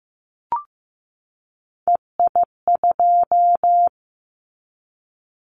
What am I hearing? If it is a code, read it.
Morse code, heard as a keyed tone, about 15 wpm, 700 Hz. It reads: EI2
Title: Callsign Prefixes for Top 50 DXCC - 15wpm